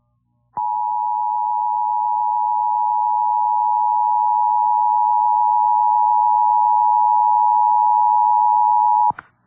Emergency Alert System (SAME) tones (the 853 and 960 Hz attention tone). It is heard from 0.5 to 9.2 s.